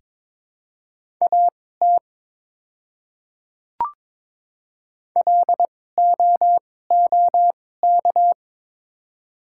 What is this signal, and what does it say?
Morse code, heard as a keyed tone, about 22 wpm, spaced as for 16 wpm, 700 Hz. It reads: AT LOOK
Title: 2x Character Spacing - Top 200 Words - 22wpm